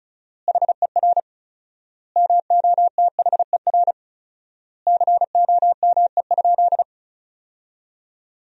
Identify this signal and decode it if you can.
Morse code, heard as a keyed tone, about 36 wpm, 700 Hz. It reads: HER MOTHER COME?
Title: 4x Word-Spacing - Sentences from Top 200 Words - 35wpm